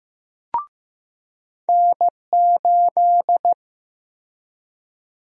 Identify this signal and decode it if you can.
Morse code, heard as a keyed tone, about 15 wpm, 700 Hz. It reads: N8
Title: USA Call Sign Prefixes - 15wpm